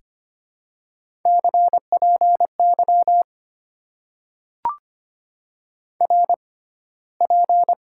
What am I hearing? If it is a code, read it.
Morse code, heard as a keyed tone, about 25 wpm, 700 Hz. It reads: C P Y R P
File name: Top 90 QSO Elements - 6x Character Spacing - 25wpm